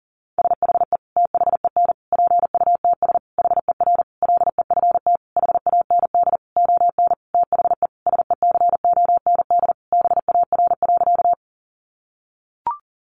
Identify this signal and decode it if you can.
Morse code, heard as a keyed tone, about 40 wpm, 700 Hz. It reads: SHE THEN PUTS HER LEFT HAND ON THE SECOND BAR.
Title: Easy Phrases 40wpm